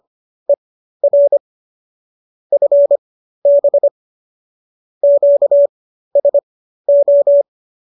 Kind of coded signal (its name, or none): Morse code